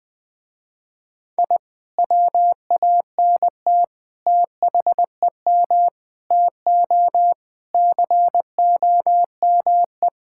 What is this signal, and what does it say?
Morse code, heard as a keyed tone, about 20 wpm, 700 Hz. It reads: I WANT THEM TO COME
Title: Sentences with 2-Word Phrases - Spoken Beginning - 20wpm